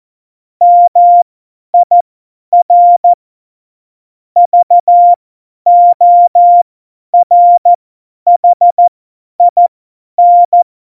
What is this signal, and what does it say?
Morse code, heard as a keyed tone, about 14 wpm, spaced as for 10 wpm, 700 Hz. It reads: MIR VORHIN
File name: Morsekurs / Witze / Witz 22. September 2020, 50 F70